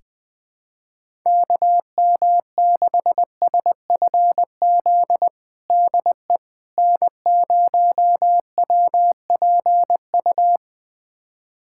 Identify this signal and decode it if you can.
Morse code, heard as a keyed tone, about 20 wpm, 700 Hz. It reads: KM6SFZ DE N0WPU
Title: USA Callsign DE Callsign 20wpm